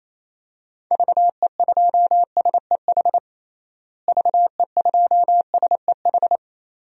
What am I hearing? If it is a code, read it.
Morse code, heard as a keyed tone, about 28 wpm, 700 Hz. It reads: VE2SEH VE2SEH
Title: DXCC-100 Call Signs - 6 Characters - Repeated 2x - 28wpm